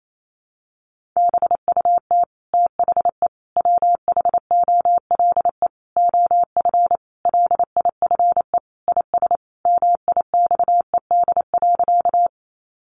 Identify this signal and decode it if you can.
Morse code, heard as a keyed tone, about 28 wpm, 700 Hz. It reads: BUT THE WHOLE OF LIFE IS MIXED.
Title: Sentences from Top 1500 Words 28wpm